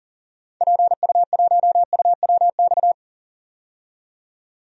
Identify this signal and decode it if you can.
Morse code, heard as a keyed tone, about 40 wpm, 700 Hz. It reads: PU1UWX
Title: DXCC-100 Call Signs - 6 Characters - 40wpm